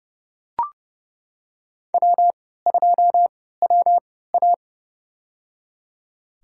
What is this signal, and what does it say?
Morse code, heard as a keyed tone, about 30 wpm, spaced as for 17 wpm, 700 Hz. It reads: W2WA
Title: USA Call Signs - 4 Characters - 3x Character Spacing - 30wpm